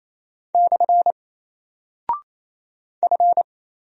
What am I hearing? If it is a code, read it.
Morse code, heard as a keyed tone, about 27 wpm, 700 Hz. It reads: / F
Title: Single Character (Letter:Number:Punctuation:Prosign) 28wpm